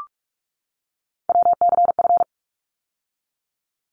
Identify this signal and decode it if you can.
Morse code, heard as a keyed tone, about 44 wpm, 700 Hz. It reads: WCF
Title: ARRL Sections 45wpm